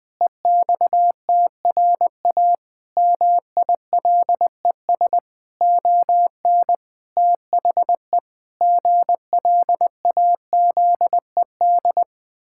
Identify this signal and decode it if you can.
Morse code, heard as a keyed tone, about 20 wpm, 700 Hz. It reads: EXTRA MILES ON THE GLAZED
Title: Pangram Challenge v2 - 20wpm